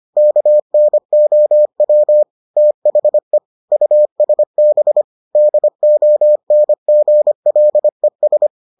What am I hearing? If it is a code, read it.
Morse code, heard as a keyed tone, about 25 wpm, 600 Hz. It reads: KNOW THE USB DONGLES